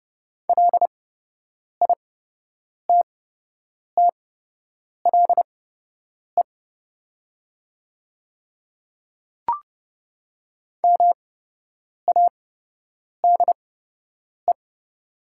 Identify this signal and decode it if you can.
Morse code, heard as a keyed tone, about 30 wpm, spaced as for 8 wpm, 700 Hz. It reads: LITTLE MADE
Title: 8x Character Spacing - Top 300 Words - 30wpm